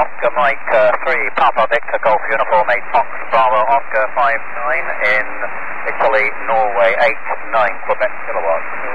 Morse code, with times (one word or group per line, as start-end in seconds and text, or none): none